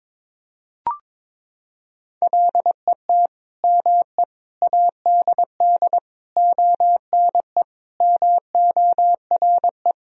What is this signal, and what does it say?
Morse code, heard as a keyed tone, about 22 wpm, 700 Hz. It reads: LET ME ADD ONE MORE
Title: Sentences from Top 1000 Words - Review Part 1 - 22wpm